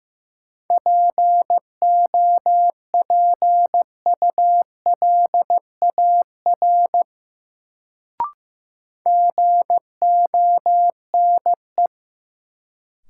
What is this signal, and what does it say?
Morse code, heard as a keyed tone, about 16 wpm, 700 Hz. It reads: POPULAR GONE
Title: Sentences from Top 1200 Words - New Word Drill Alt - 15wpm